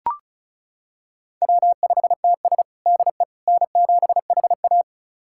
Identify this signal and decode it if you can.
Morse code, heard as a keyed tone, about 36 wpm, 700 Hz. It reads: W5TS DE N7HA
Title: USA Callsign DE Callsign - 4 Characters - 35wpm